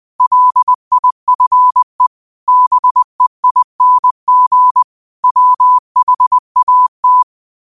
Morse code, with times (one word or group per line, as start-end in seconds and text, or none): 0.2-2.1 LIFE
2.5-4.8 BEING
5.2-7.2 WHAT